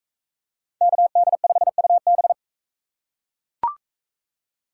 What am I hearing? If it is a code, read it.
Morse code, heard as a keyed tone, about 37 wpm, 700 Hz. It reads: KD5UB